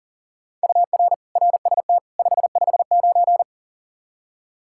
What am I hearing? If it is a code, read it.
Morse code, heard as a keyed tone, about 40 wpm, 700 Hz. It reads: UR RST 559
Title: Practice QSO - Exchange 1 - Beginning - 40wpm